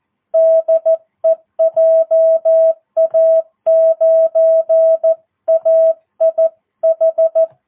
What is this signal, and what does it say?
Morse code, heard as a keyed tone, about 14 wpm, 650 Hz. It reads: DE JA9AIH
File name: jh8cbh / 2022cw